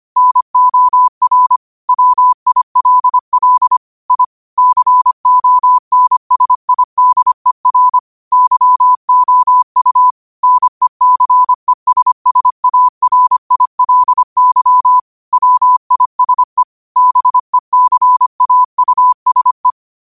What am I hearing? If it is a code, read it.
Morse code, heard as a keyed tone, about 25 wpm, 1000 Hz. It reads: NOR WILL I CONSIDER YOU NECESSARILY WISE BECAUSE
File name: qotdfiles / 25wpm